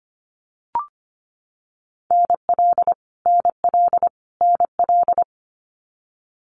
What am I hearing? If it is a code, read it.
Morse code, heard as a keyed tone, about 25 wpm, 700 Hz. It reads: NL NL NL
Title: Canadian Provinces and Territories - Repeated 3x - 25wpm